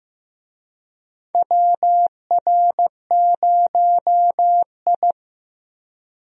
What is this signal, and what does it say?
Morse code, heard as a keyed tone, about 16 wpm, 700 Hz. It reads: WR0I
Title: USA Call Signs - 4 Characters - Repeated 3x - 15wpm